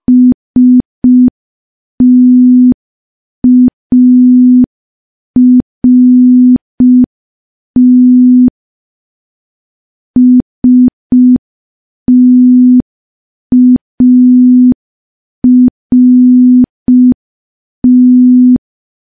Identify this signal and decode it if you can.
Morse code, heard as a keyed tone, about 5 wpm, 250 Hz. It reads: START START
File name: morse / morse-training-05WPM